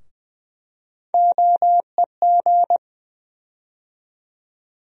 Morse code, heard as a keyed tone, about 21 wpm, 700 Hz. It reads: OEG
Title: USA Call Sign Postfixes - 3 Letters 20wpm